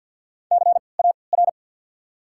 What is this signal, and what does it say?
Morse code, heard as a keyed tone, about 54 wpm, spaced as for 29 wpm, 700 Hz. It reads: CAR
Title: 3x Character Spacing - Top 300 Words - 50wpm